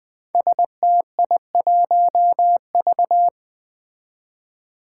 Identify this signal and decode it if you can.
Morse code, heard as a keyed tone, about 20 wpm, 700 Hz. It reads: STI1V
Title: Five Challenging Letter-Number Combinations 20wpm